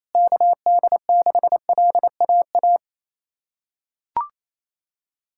Morse code, heard as a keyed tone, about 28 wpm, 700 Hz. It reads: KD6LAA
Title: USA Callsign DE Callsign - 6 Characters - 28wpm